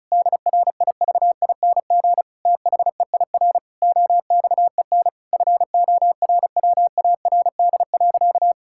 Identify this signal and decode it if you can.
Morse code, heard as a keyed tone, about 34 wpm, 700 Hz. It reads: DRIVING THEIR OXEN FORWARD.
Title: Pangram Challenge v2 - 35wpm